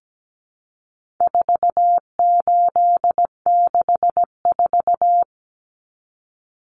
Morse code, heard as a keyed tone, about 17 wpm, 700 Hz. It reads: 4864
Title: Numbers - 4-Digit 17wpm